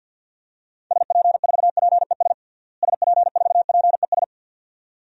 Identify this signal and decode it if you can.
Morse code, heard as a keyed tone, about 50 wpm, 700 Hz. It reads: SP4PES SP4PES
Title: DXCC-100 Call Signs - 6 Characters - Repeated 2x - 50wpm